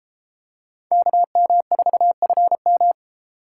Morse code, heard as a keyed tone, about 31 wpm, 700 Hz. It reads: KM4FM